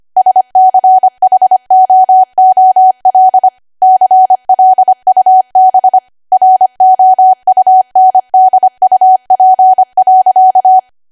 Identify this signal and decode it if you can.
Morse code, heard as a keyed tone, about 25 wpm, 750 Hz. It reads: SCHOOL CLUB ROUNDUP.
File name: Archive / 25 WPM / 25WPM